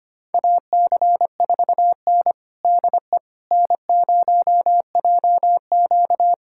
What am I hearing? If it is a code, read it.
Morse code, heard as a keyed tone, about 25 wpm, 700 Hz. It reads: AC4N DE N0JQ
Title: USA Callsign DE Callsign - 4 Characters - 25wpm